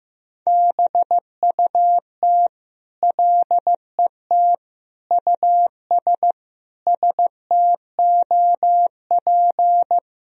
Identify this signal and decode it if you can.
Morse code, heard as a keyed tone, about 16 wpm, 700 Hz. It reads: BUT LET US STOP
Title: Sentences from Top 1000 Words - Review Part 1 - 15wpm